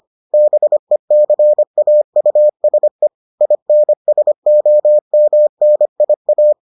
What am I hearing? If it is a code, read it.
Morse code, heard as a keyed tone, about 25 wpm, 600 Hz. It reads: BECAUSE INSOMNIA